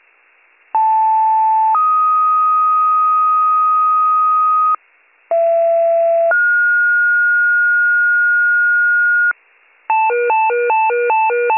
Two-tone page, heard as a one-second tone, then a three-second tone, about 880 then 1250 Hz, about 670 then 1430 Hz, with noise. It is heard from 0.7 to 9.3 s.